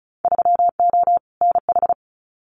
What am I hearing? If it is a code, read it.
Morse code, heard as a keyed tone, about 34 wpm, 700 Hz. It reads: 3O NH
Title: Winter Field Day Exchanges Encore 35wpm